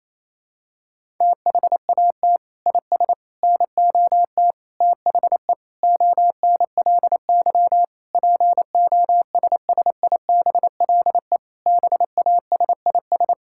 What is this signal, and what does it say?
Morse code, heard as a keyed tone, about 28 wpm, 700 Hz. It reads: THAT IS NOT THE ONLY POSSIBLE BASIS
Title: Sentences from Top 1000 Words - Review Part 5 - 28wpm